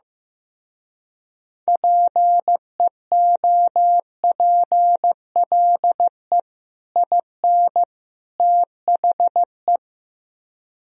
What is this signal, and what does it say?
Morse code, heard as a keyed tone, about 15 wpm, 700 Hz. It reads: PEOPLE IN THE
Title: Speed-Racing - Sets of 3 Words 15wpm